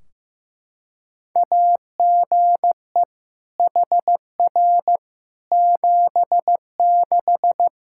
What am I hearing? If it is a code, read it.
Morse code, heard as a keyed tone, about 15 wpm, 700 Hz. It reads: AGE HR 76
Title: Practice QSO - Exchange 3 - More Info - 15wpm